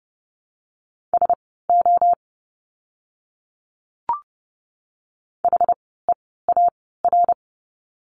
Morse code, heard as a keyed tone, about 30 wpm, spaced as for 17 wpm, 700 Hz. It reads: SO HEAR
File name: course / 30:10wpm / 230 Lesson 37f - Letter x - Words 30wpm